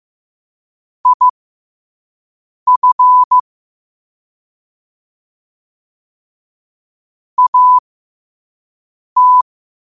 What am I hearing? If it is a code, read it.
Morse code, heard as a keyed tone, about 15 wpm, spaced as for 5 wpm, 1000 Hz. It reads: IF AT
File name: qotdfiles / 5wpm